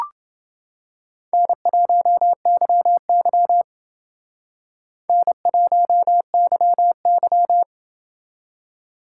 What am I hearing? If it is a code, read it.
Morse code, heard as a keyed tone, about 28 wpm, 700 Hz. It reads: N1YY N1YY